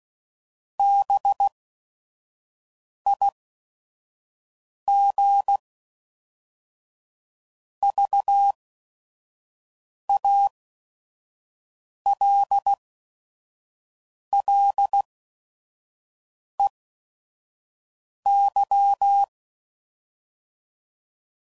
Morse code, heard as a keyed tone, about 16 wpm, spaced as for 5 wpm, 800 Hz. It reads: BIG VALLEY
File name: CW Code Groups @5WPM / fromSM's / Cities